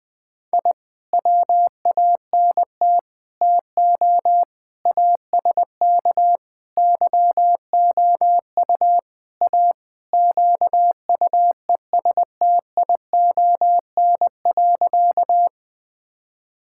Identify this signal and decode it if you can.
Morse code, heard as a keyed tone, about 20 wpm, 700 Hz. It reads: I WANT TO ASK YOU A QUESTION.
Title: Sentences with 5-Word Phrases - 20wpm